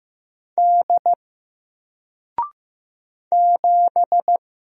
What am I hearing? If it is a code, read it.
Morse code, heard as a keyed tone, about 15 wpm, 700 Hz. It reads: D 7
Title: Single Letter-Number Alt 15wpm